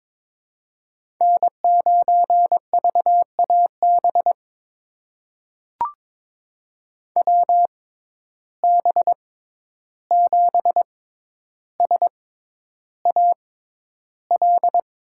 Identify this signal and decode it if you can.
Morse code, heard as a keyed tone, about 22 wpm, spaced as for 11 wpm, 700 Hz. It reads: N9VAB WB7SAL
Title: USA Call Signs - 6x Character Spacing - 22wpm